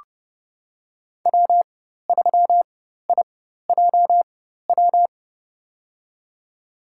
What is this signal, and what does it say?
Morse code, heard as a keyed tone, about 30 wpm, spaced as for 14 wpm, 700 Hz. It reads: W3IJW
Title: USA Call Signs - 5 Characters - 4x Character Spacing - 30wpm